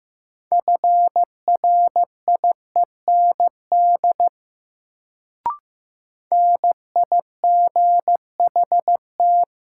Morse code, heard as a keyed tone, about 15 wpm, 700 Hz. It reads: FRIEND NIGHT
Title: Sentences from Top 300 Words - New Word Drill - 15wpm